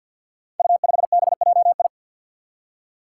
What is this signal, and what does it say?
Morse code, heard as a keyed tone, about 46 wpm, 700 Hz. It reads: U5BJI
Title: Five Challenging Letter-Number Combinations 50wpm